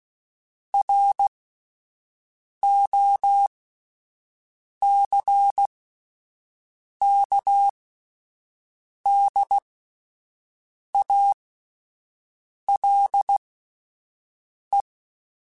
Morse code, heard as a keyed tone, about 16 wpm, spaced as for 5 wpm, 800 Hz. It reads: ROCKDALE